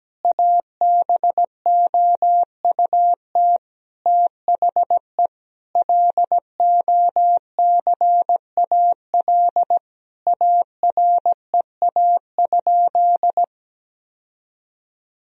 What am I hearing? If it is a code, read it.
Morse code, heard as a keyed tone, about 17 wpm, 700 Hz. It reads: ABOUT THE LOCAL AREA?
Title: Sentences from Top 1000 Words - Review Part 3 - 17wpm